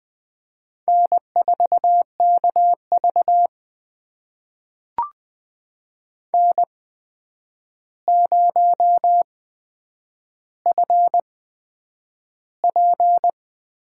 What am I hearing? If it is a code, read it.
Morse code, heard as a keyed tone, about 20 wpm, spaced as for 9 wpm, 700 Hz. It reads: N4KV N0FP
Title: USA Call Signs - 4 Characters - 8x Character Spacing - 20wpm